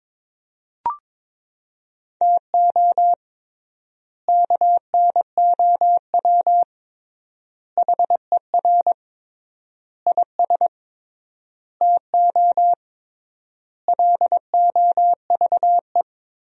Morse code, heard as a keyed tone, about 22 wpm, 700 Hz. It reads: TO KNOW HER IS TO LOVE
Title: 3x Word-Spacing - Sentences from Top 400 Words - 22wpm